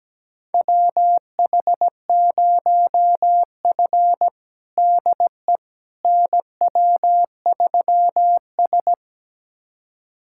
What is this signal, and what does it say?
Morse code, heard as a keyed tone, about 17 wpm, 700 Hz. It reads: WH0F DE NW3S
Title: USA Callsign DE Callsign - 4 Characters - 17wpm